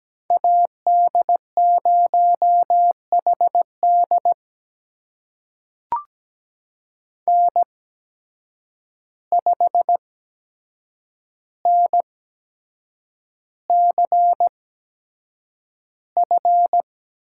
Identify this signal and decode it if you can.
Morse code, heard as a keyed tone, about 17 wpm, spaced as for 7 wpm, 700 Hz. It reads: AD0HD N5NCF
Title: USA Call Signs - 8x Character Spacing - 17wpm